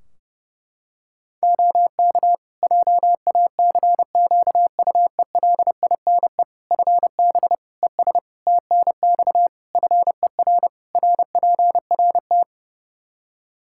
Morse code, heard as a keyed tone, about 30 wpm, 700 Hz. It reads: OK JACQUELINE FB ES TNX FER RPRT
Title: K1USN SST QSO Practice 30wpm